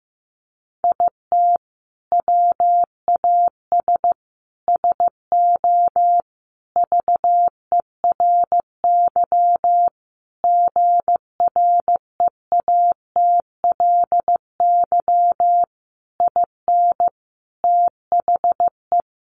Morse code, heard as a keyed tone, about 15 wpm, 700 Hz. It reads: IT WAS SO VERY GREATLY IN THE
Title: Sentences from Top 1400 Words 15wpm